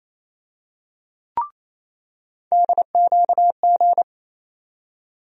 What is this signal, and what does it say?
Morse code, heard as a keyed tone, about 28 wpm, 700 Hz. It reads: DQG
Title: USA Call Sign Postfixes - 3 Letters 28wpm